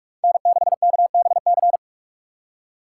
Morse code, heard as a keyed tone, about 58 wpm, spaced as for 51 wpm, 700 Hz. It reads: N6KBC
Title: Speed Racing - USA 1x3 Call Signs 45wpm